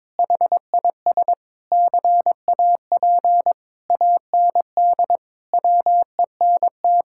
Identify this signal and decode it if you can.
Morse code, heard as a keyed tone, about 22 wpm, 700 Hz. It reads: HIS CAP AND WENT
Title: Sentences from Top 1600 Words 22wpm